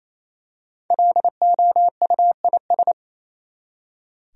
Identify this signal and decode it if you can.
Morse code, heard as a keyed tone, about 28 wpm, 700 Hz. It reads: LOUIS
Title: USA Common Names - 28wpm